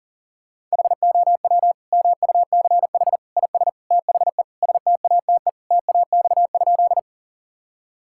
Morse code, heard as a keyed tone, about 40 wpm, 700 Hz. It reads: HOW MUCH IS THE STATE TAX?
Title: Sentences from Top 1000 Words - Review Part 5 - 40wpm